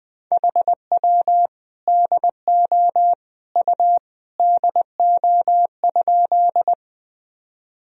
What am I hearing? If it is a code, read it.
Morse code, heard as a keyed tone, about 20 wpm, 700 Hz. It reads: HW DO U DO?
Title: Sentences from Top 100 Words and Abbreviations 20wpm